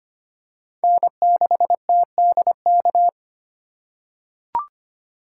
Morse code, heard as a keyed tone, about 25 wpm, 700 Hz. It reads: N6TDK
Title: USA Call Signs - 5 Characters - 4x Character Spacing - 25wpm